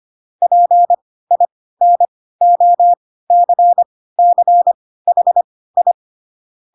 Morse code, heard as a keyed tone, about 25 wpm, spaced as for 16 wpm, 700 Hz. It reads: PINOCCHI
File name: Pinocchio / Pinocchio ch3 10 25 ik1hge